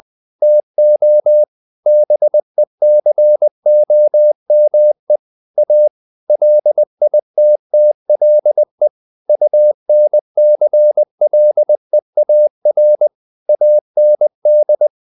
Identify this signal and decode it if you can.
Morse code, heard as a keyed tone, about 20 wpm, 600 Hz. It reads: TO BECOME A LITTLE UNCLEAR AND
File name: the old man and the sea part 2 20wpm